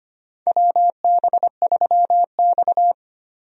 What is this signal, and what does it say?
Morse code, heard as a keyed tone, about 25 wpm, 700 Hz. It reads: WB3X
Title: USA Call Signs - 4 Characters - Repeated 3x - 25wpm